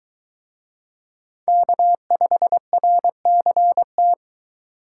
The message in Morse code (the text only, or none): K5RCT